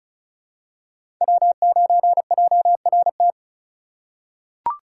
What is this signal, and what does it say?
Morse code, heard as a keyed tone, about 36 wpm, 700 Hz. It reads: W9JRT
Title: USA Call Signs - 5 Characters - 3x Character Spacing - 35wpm